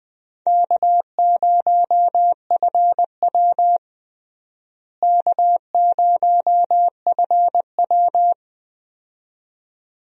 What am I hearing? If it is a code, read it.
Morse code, heard as a keyed tone, about 21 wpm, 700 Hz. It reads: K0FW K0FW
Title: USA Call Signs - 4 Characters - Repeated 2x - 20wpm